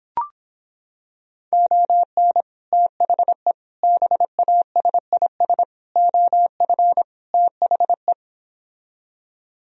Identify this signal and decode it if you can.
Morse code, heard as a keyed tone, about 26 wpm, 700 Hz. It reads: ON THE BASIS OF THE